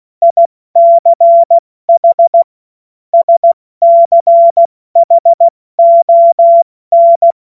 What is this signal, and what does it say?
Morse code, heard as a keyed tone, about 16 wpm, spaced as for 14 wpm, 680 Hz. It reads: ICH SCHON
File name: Morsekurs / Witze / Witz 30. Mai 2023 70 F80